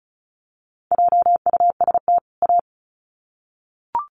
Morse code, heard as a keyed tone, about 34 wpm, 700 Hz. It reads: JUST A
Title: Sets of 2 Words 35wpm